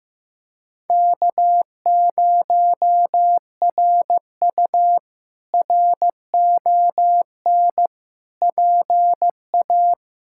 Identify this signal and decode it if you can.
Morse code, heard as a keyed tone, about 15 wpm, 700 Hz. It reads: K0RU RON PA
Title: K1USN SST QSO Practice - 15wpm